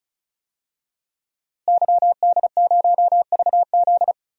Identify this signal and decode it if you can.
Morse code, heard as a keyed tone, about 36 wpm, 700 Hz. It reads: YD0VZ
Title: DXCC-100 Call Signs - 5 Characters - Repeated 2x - 35wpm